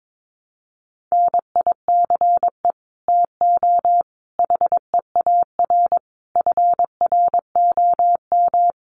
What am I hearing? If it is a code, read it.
Morse code, heard as a keyed tone, about 22 wpm, 700 Hz. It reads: NICE TO HEAR FROM
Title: Common QSO Phrases 22wpm